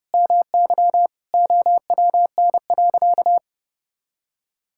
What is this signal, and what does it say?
Morse code, heard as a keyed tone, about 32 wpm, 700 Hz. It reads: MY OWN.
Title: Sentences from Top 1000 Words - Review Part 6 - 30wpm